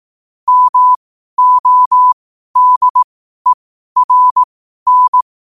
Morse code, heard as a keyed tone, about 18 wpm, spaced as for 12 wpm, 1000 Hz. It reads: MODERN
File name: qotdfiles / 12wpm